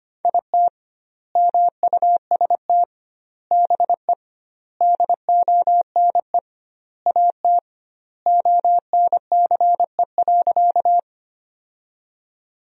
Morse code, heard as a keyed tone, about 25 wpm, 700 Hz. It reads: IT MUST BE DONE AT ONCE.
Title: 2x Word-Spacing - Sentences from Top 500 Words - 25wpm